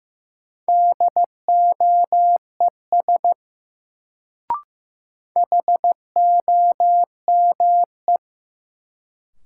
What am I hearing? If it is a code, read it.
Morse code, heard as a keyed tone, about 15 wpm, 700 Hz. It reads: DOES HOME
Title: Sentences from Top 200 Words - New Word Drill - 15wpm